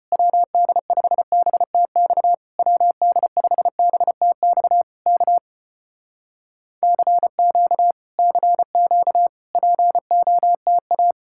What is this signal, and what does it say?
Morse code, heard as a keyed tone, about 35 wpm, 700 Hz. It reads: WD5BTX WD5BTX K CQ CQ POTA